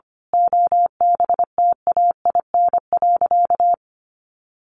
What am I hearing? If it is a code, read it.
Morse code, heard as a keyed tone, about 25 wpm, 700 Hz. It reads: OBTAIN.
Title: Sentences from Top 1300 Words 25wpm